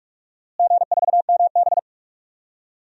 Morse code, heard as a keyed tone, about 47 wpm, 700 Hz. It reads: G4MB
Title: DXCC-100 Call Signs - 4 Characters - Repeated 3x - 45wpm